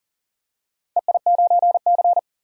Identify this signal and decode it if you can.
Morse code, heard as a keyed tone, about 40 wpm, 700 Hz. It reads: EI9C
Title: DXCC-100 Call Signs - 4 Characters - 40wpm